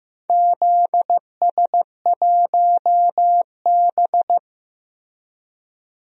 Morse code, heard as a keyed tone, about 15 wpm, 700 Hz. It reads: ZS1B
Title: DXCC-100 Call Signs - 4 Characters - Repeated 2x - 15wpm